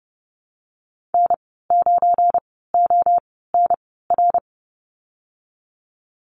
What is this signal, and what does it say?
Morse code, heard as a keyed tone, about 30 wpm, spaced as for 17 wpm, 700 Hz. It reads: N9ONR
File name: Lesson 37g - Letter x - Call Signs 30wpm